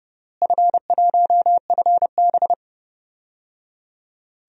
Morse code, heard as a keyed tone, about 30 wpm, 700 Hz. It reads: F1FB
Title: DXCC-100 Call Signs - 4 Characters - Repeated 3x - 30wpm